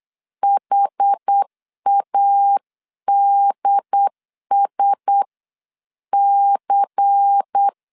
Morse code, heard as a keyed tone, about 8 wpm, spaced as for 7 wpm, 800 Hz. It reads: HADSC
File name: CW / DET / 1 6 1500C S B 8wpm 5142